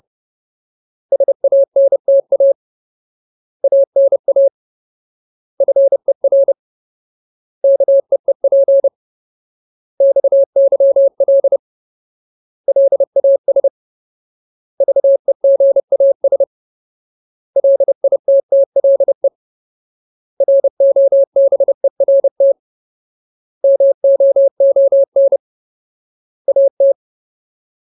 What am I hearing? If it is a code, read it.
Morse code, heard as a keyed tone, about 30 wpm, 550 Hz. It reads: SANTA ANA FER KEEP XYL LAS VEGAS LITTLE ROBERT MOON AT
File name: Ultimate Word List / 30WPM Ultimate